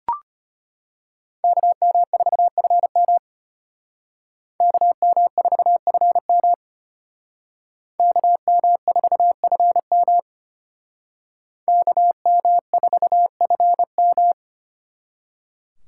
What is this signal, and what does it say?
Morse code, heard as a keyed tone, about 32 wpm, 700 Hz. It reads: KM4FM KM4FM KM4FM KM4FM